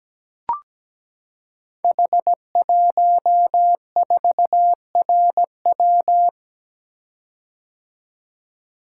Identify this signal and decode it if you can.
Morse code, heard as a keyed tone, about 17 wpm, 700 Hz. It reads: H14RW
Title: Code Groups - 5 Letter:Numbers - 17wpm